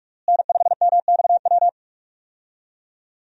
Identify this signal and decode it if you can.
Morse code, heard as a keyed tone, about 42 wpm, 700 Hz. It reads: N5MXW
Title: USA Call Signs - 5 Characters - Repeated 3x - 45wpm